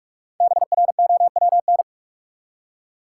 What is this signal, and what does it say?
Morse code, heard as a keyed tone, about 44 wpm, 700 Hz. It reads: BROWN
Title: Five Letter Words - 45wpm